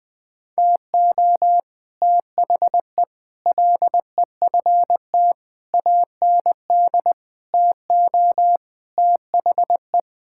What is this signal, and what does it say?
Morse code, heard as a keyed tone, about 20 wpm, 700 Hz. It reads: TO THE LEFT AND TO THE
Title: Sets of 9 Words - 20wpm